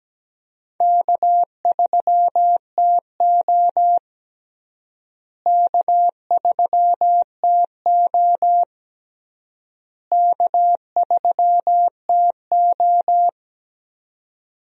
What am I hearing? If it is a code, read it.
Morse code, heard as a keyed tone, about 17 wpm, 700 Hz. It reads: K3TO K3TO K3TO
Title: USA Call Signs - 4 Characters - Repeated 3x - 17wpm